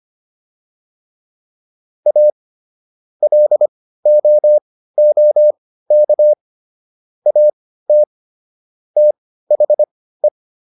Morse code, heard as a keyed tone, about 25 wpm, spaced as for 15 wpm, 600 Hz. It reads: A LOOK AT THE